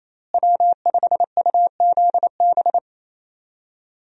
Morse code, heard as a keyed tone, about 28 wpm, 700 Hz. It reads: W5UZB
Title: USA Call Signs - 5 Characters - Repeated 3x - 28wpm